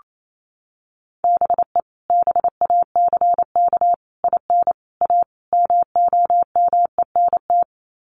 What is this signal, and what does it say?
Morse code, heard as a keyed tone, about 28 wpm, 700 Hz. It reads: BE BACK IN A MOMENT
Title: Sets of 5 Words 28wpm